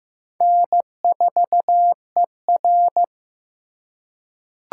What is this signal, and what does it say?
Morse code, heard as a keyed tone, about 15 wpm, 700 Hz. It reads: N4ER
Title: USA Call Signs - 4 Characters - 4x Character Spacing - 15wpm